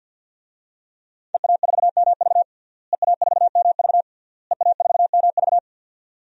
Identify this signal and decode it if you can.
Morse code, heard as a keyed tone, about 52 wpm, 700 Hz. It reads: EA4MV EA4MV EA4MV
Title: DXCC-100 Call Signs - 5 Characters - Repeated 3x - 50wpm